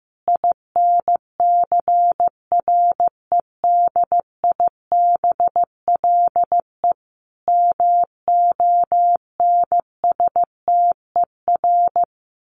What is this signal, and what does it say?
Morse code, heard as a keyed tone, about 15 wpm, 700 Hz. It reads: INCREDIBLE MONSTER
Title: A Sound Of Thunder 15wpm